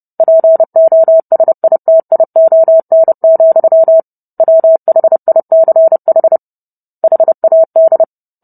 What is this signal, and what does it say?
Morse code, heard as a keyed tone, about 30 wpm, 650 Hz. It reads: POSITION, WHICH HAD